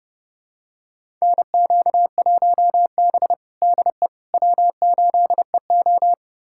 Morse code, heard as a keyed tone, about 32 wpm, 700 Hz. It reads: NQ1B DE W8EO
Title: USA Callsign DE Callsign - 4 Characters - 30wpm